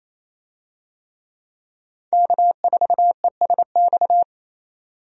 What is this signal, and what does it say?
Morse code, heard as a keyed tone, about 27 wpm, 700 Hz. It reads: K4ESX
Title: USA Call Signs - 5 Characters - 28wpm